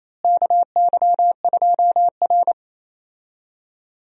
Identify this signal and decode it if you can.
Morse code, heard as a keyed tone, about 28 wpm, 700 Hz. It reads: KY2R
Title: USA Call Signs - 4 Characters - Repeated 3x - 28wpm